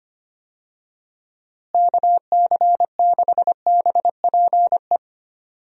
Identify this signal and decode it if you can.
Morse code, heard as a keyed tone, about 25 wpm, 700 Hz. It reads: KC6BPE
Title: USA Call Signs - 6 Characters - Repeated 2x - 25wpm